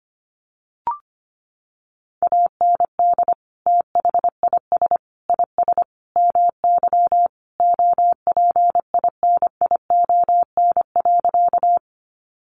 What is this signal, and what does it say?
Morse code, heard as a keyed tone, about 25 wpm, 700 Hz. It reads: AND THIS IS MY OPINION.
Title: Sentences from Top 1200 Words 25wpm